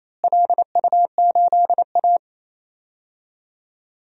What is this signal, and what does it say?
Morse code, heard as a keyed tone, about 28 wpm, 700 Hz. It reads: LU8A
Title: DXCC-100 Call Signs - 4 Characters - Repeated 3x - 28wpm